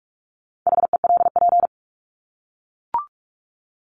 Morse code, heard as a keyed tone, about 44 wpm, 700 Hz. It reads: HELP
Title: Top 200 Words 45wpm